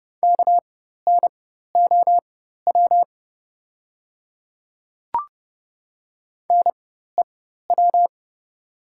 Morse code, heard as a keyed tone, about 30 wpm, spaced as for 14 wpm, 700 Hz. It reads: KNOW NEW
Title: 4x Character Spacing - Top 300 Words - 30wpm